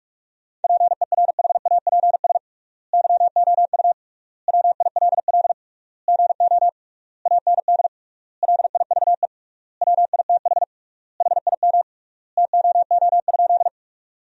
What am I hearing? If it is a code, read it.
Morse code, heard as a keyed tone, about 43 wpm, 700 Hz. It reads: PERHAPS YOU WILL GO AND LIVE WITH HIM TOO?
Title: 3x Word-Spacing - Sentences from Top 500 Words - 45wpm